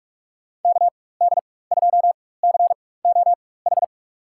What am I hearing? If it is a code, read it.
Morse code, heard as a keyed tone, about 44 wpm, spaced as for 21 wpm, 700 Hz. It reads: KD2COH
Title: USA Call Signs - 6 Characters - 4x Character Spacing - 45wpm